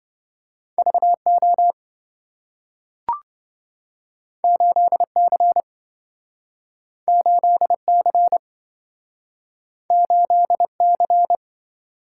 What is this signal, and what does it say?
Morse code, heard as a keyed tone, about 27 wpm, 700 Hz. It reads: VO 8C 8C 8C